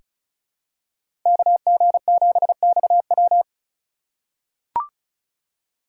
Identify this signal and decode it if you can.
Morse code, heard as a keyed tone, about 36 wpm, 700 Hz. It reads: KG7XW
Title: USA Call Signs - 5 Characters - 2x Character Spacing - 35wpm